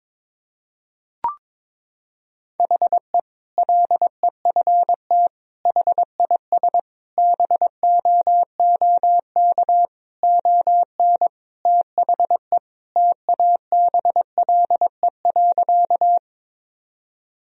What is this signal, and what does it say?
Morse code, heard as a keyed tone, about 22 wpm, 700 Hz. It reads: HE LEFT HIS BOOK ON THE TABLE.
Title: Sentences from Top 1000 Words - Review Part 2 - 22wpm